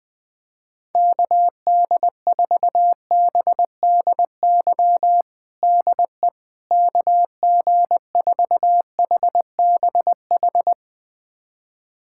Morse code, heard as a keyed tone, about 20 wpm, 700 Hz. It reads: KD4BDY DE KG4HBH
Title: USA Callsign DE Callsign 20wpm